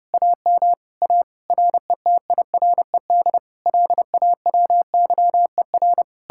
Morse code, heard as a keyed tone, about 40 wpm, 700 Hz. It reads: AM A RETIRED LAWYER